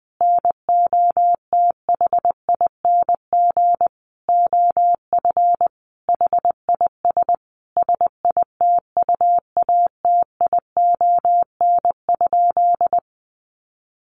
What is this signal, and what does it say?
Morse code, heard as a keyed tone, about 20 wpm, 700 Hz. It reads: NOTHING OF HIS SITUATION?
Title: Sentences from Top 1200 Words 20wpm